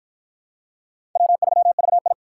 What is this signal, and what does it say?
Morse code, heard as a keyed tone, about 49 wpm, 700 Hz. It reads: W3VI